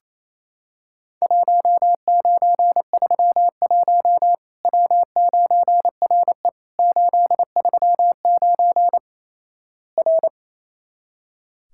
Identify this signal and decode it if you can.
Morse code, heard as a keyed tone, about 28 wpm, 700 Hz. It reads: 1931 W9RE 839 R
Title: Worked All Europe DX Contest - QTC Exchange - 28wpm